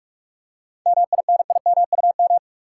Morse code, heard as a keyed tone, about 44 wpm, 700 Hz. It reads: MINIMUM
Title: Seven Letter Words - 45wpm